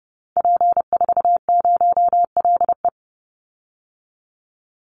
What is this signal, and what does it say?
Morse code, heard as a keyed tone, about 30 wpm, 700 Hz. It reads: P40LE
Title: Call Signs Repeated 2x - 30wpm